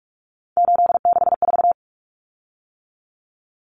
Morse code, heard as a keyed tone, about 44 wpm, 700 Hz. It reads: 864
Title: Common Cut Numbers - 3-Digit 45wpm